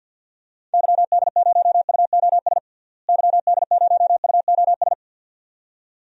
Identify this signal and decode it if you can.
Morse code, heard as a keyed tone, about 53 wpm, 700 Hz. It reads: YD0UOS YD0UOS
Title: DXCC-100 Call Signs - 6 Characters - Repeated 2x - 50wpm